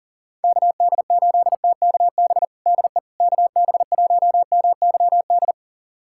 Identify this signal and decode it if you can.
Morse code, heard as a keyed tone, about 40 wpm, 690 Hz. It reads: KD8TKB DE KB1MYD
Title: USA Callsign DE Callsign - 6 Characters - 40wpm